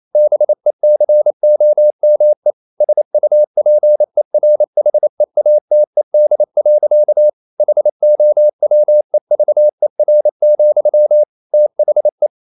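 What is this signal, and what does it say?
Morse code, heard as a keyed tone, about 28 wpm, 600 Hz. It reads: BECOME SUPERHEATED. HOWEVER, THE
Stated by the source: Morse code practice at 28 wpm, text of Ken Follett's novel Never in English, from CW med Gnister.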